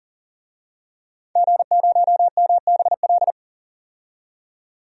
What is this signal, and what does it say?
Morse code, heard as a keyed tone, about 38 wpm, 700 Hz. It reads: G0MBL
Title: DXCC-100 Call Signs - 5 Characters - 40wpm